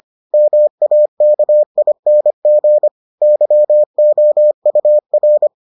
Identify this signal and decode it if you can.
Morse code, heard as a keyed tone, about 25 wpm, 600 Hz. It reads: MAKING YOUR